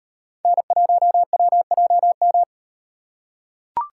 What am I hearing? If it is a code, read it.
Morse code, heard as a keyed tone, about 36 wpm, 700 Hz. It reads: N1WJM